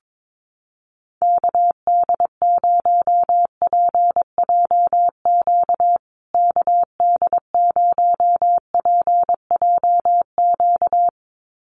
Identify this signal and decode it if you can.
Morse code, heard as a keyed tone, about 22 wpm, 700 Hz. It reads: KD0PJQ KD0PJQ
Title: USA Call Signs Repeated 2x 22wpm